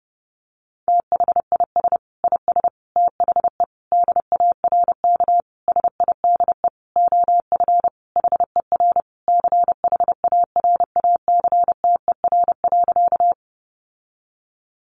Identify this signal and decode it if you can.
Morse code, heard as a keyed tone, about 30 wpm, 700 Hz. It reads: THIS IS THE DARK SIDE OF HER CHARACTER.
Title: Sentences from Top 1200 Words 30wpm